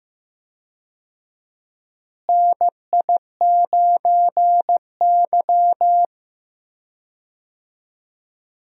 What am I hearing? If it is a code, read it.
Morse code, heard as a keyed tone, about 15 wpm, 700 Hz. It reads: NI9Y
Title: USA Call Signs - 4 Characters - Repeated 3x - 15wpm